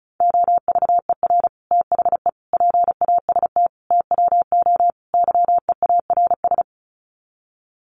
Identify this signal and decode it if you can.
Morse code, heard as a keyed tone, about 34 wpm, 700 Hz. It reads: OVER THE PAST TWO YEARS
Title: Sets of 5 Words Encore 35wpm